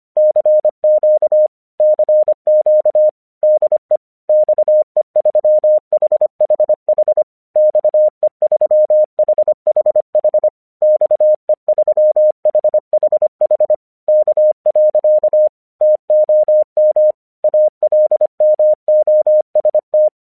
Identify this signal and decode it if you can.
Morse code, heard as a keyed tone, about 25 wpm, 610 Hz. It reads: CQ CQ DE XE3HHH XE3HHH XE3HHH K. TOM ALMOST
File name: 1 English 25wpm / AQC30